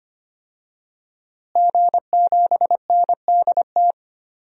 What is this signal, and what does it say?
Morse code, heard as a keyed tone, about 25 wpm, 700 Hz. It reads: G7NDT
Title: DXCC-100 Call Signs - 5 Characters - Repeated 2x - 25wpm